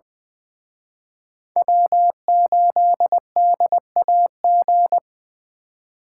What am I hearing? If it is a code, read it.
Morse code, heard as a keyed tone, about 24 wpm, 700 Hz. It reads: W8DAG
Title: Speed Racing - USA 1x3 Call Signs 20wpm